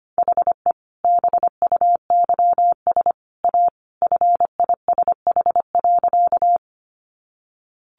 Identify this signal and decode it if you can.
Morse code, heard as a keyed tone, about 25 wpm, 700 Hz. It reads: HE BUYS A FISH.
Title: Additional Easy Phrases 25wpm